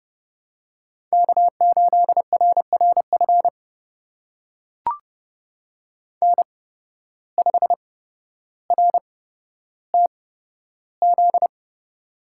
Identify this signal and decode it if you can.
Morse code, heard as a keyed tone, about 30 wpm, spaced as for 13 wpm, 700 Hz. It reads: K8RRF N5RTZ
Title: USA Call Signs - 5 Characters - 8x Character Spacing - 30wpm